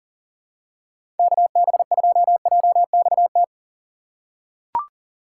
Morse code, heard as a keyed tone, about 38 wpm, 700 Hz. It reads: KB2JXT
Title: USA Call Signs - 6x Character Spacing - 40wpm